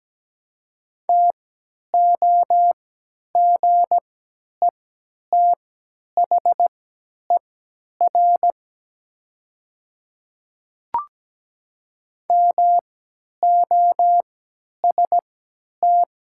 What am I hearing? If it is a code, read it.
Morse code, heard as a keyed tone, about 17 wpm, spaced as for 10 wpm, 700 Hz. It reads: TOGETHER MOST
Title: 3x Character Spacing - Top 300 Words - 17wpm